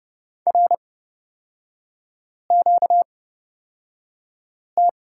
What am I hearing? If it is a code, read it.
Morse code, heard as a keyed tone, about 30 wpm, spaced as for 5 wpm, 700 Hz. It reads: RQT
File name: Lesson 38e - Letter q - Three Characters 30wpm